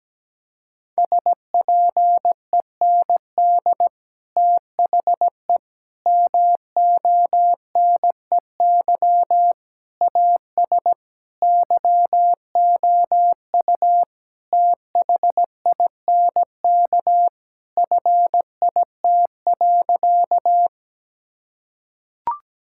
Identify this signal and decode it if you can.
Morse code, heard as a keyed tone, about 17 wpm, 700 Hz. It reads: SPEND THE MONEY AS YOU THINK FIT.
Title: Sentences from Top 1000 Words - Review Part 5 - 17wpm